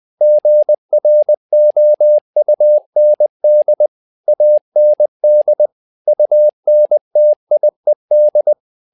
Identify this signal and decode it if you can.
Morse code, heard as a keyed tone, about 20 wpm, 600 Hz. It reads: GROUND AND UNTIED
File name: learning center / tower trouble / tower trouble 20wpm